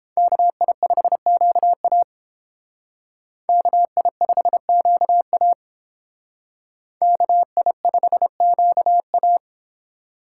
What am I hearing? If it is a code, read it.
Morse code, heard as a keyed tone, about 30 wpm, 700 Hz. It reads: KI5QA KI5QA KI5QA